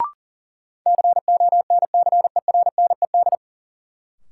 Morse code, heard as a keyed tone, about 40 wpm, 700 Hz. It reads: CONCERNED
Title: Sentences from Top 1000 Words - Review Part 6 - New Word Drill - 40wpm